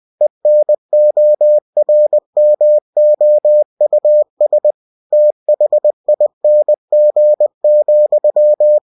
Morse code, heard as a keyed tone, about 20 wpm, 600 Hz. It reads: ENORMOUS THING,
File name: cw-books / winnie-the-pooh / winnie-20wpm-page-0043